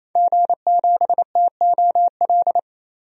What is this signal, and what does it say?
Morse code, heard as a keyed tone, about 28 wpm, 700 Hz. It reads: G7TOL
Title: DXCC-100 Call Signs - 5 Characters - 28wpm